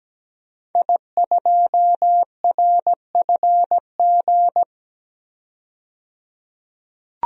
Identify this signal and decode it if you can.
Morse code, heard as a keyed tone, about 17 wpm, 700 Hz. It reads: I2RFG
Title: DXCC-100 Call Signs - 5 Characters - 17wpm